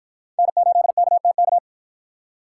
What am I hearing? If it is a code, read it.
Morse code, heard as a keyed tone, about 51 wpm, 700 Hz. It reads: N8KTK